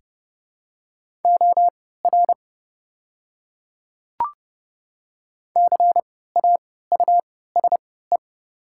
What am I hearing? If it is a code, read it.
Morse code, heard as a keyed tone, about 30 wpm, spaced as for 17 wpm, 700 Hz. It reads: OR CAUSE
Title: 3x Character Spacing - Top 300 Words - 30wpm